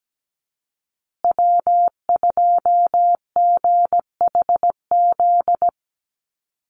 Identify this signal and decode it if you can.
Morse code, heard as a keyed tone, about 17 wpm, 700 Hz. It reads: W2GHZ
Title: USA Call Signs Repeated 2x 17wpm